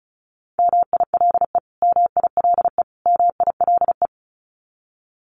Morse code, heard as a keyed tone, about 34 wpm, 700 Hz. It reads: MILE MILE MILE
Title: Top 500 Words Repeated 35wpm